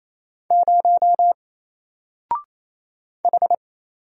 Morse code, heard as a keyed tone, about 27 wpm, 700 Hz. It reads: 0 H
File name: Single-Letter-Number-28wpm